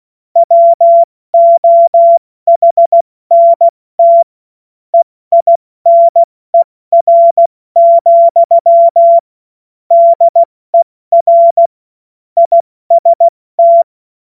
Morse code, heard as a keyed tone, about 16 wpm, 680 Hz. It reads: WOHNT EINER, DER IST
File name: Morsekurs / Witze / Witz 14. September 2021, 70 F80